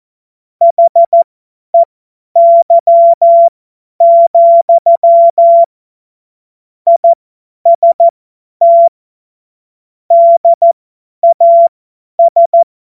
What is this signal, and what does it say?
Morse code, heard as a keyed tone, about 14 wpm, spaced as for 10 wpm, 680 Hz. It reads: HEY, IST DAS
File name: Morsekurs / Witze / Witz 14. März 2023 50 F70